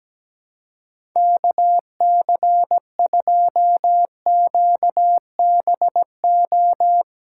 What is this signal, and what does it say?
Morse code, heard as a keyed tone, about 17 wpm, 700 Hz. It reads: KC2QBO
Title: USA Call Signs - 6 Characters - 4x Character Spacing - 17wpm